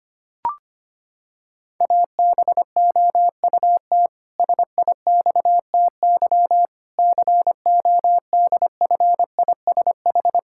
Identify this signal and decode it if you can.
Morse code, heard as a keyed tone, about 25 wpm, 700 Hz. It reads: ABOUT SIXTY CODFISH
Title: Pangram Challenge v2 - 25wpm